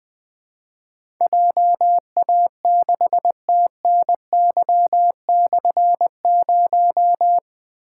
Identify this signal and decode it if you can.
Morse code, heard as a keyed tone, about 20 wpm, 700 Hz. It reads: JA6TNY/0
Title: Difficult Call Signs 20wpm